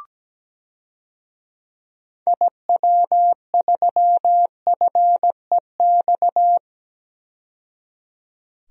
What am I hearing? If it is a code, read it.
Morse code, heard as a keyed tone, about 17 wpm, 700 Hz. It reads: IW3FEX
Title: DXCC-100 Call Signs - 6 Characters - 17wpm